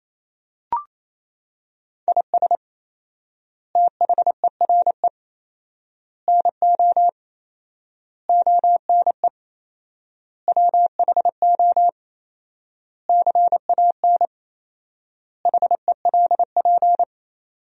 Morse code, heard as a keyed tone, about 28 wpm, 700 Hz. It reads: IS THERE NO ONE WHO CAN HELP
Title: 4x Word-Spacing - Sentences from Top 200 Words - 28wpm